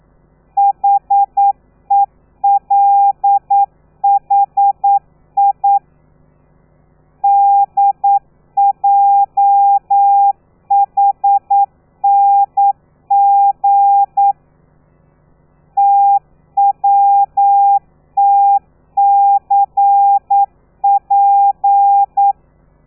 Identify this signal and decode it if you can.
Morse code, heard as a keyed tone, about 9 wpm, 800 Hz. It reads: HELHI DJHNG TWTCP
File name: telegrafia / i2tfj / Lez13 40